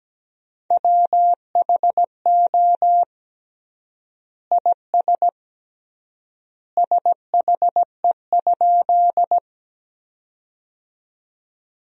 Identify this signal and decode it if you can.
Morse code, heard as a keyed tone, about 17 wpm, 700 Hz. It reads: WHO IS SHE?
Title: 3x Word-Spacing - Sentences from Top 100 Words - 17wpm